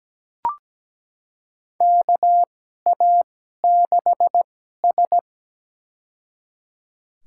Morse code, heard as a keyed tone, about 17 wpm, spaced as for 12 wpm, 700 Hz. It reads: KA6S
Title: USA Call Signs - 4 Characters - 2x Character Spacing - 17wpm